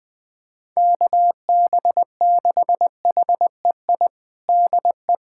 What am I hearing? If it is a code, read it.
Morse code, heard as a keyed tone, about 20 wpm, 700 Hz. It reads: KB6HEI DE
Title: USA Callsign DE Callsign - 6 Characters - 20wpm